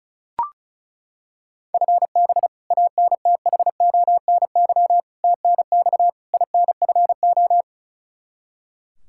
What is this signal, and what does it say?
Morse code, heard as a keyed tone, about 36 wpm, 700 Hz. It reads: FB ANTHONY TNX INFO
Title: K1USN SST QSO Practice 35wpm